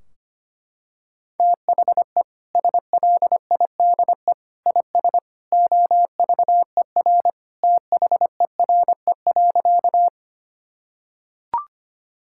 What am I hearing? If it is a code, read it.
Morse code, heard as a keyed tone, about 25 wpm, 700 Hz. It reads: THE SLIDE IS OVER THERE.
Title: Sentences from Top 1600 Words 25wpm